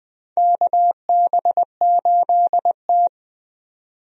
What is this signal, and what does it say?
Morse code, heard as a keyed tone, about 20 wpm, 700 Hz. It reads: KB8T
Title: USA Call Signs - 4 Characters - 20wpm